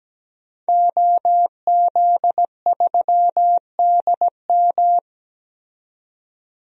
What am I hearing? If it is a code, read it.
Morse code, heard as a keyed tone, about 17 wpm, 700 Hz. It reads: OZ3DM
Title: DXCC-100 Call Signs - 5 Characters - 17wpm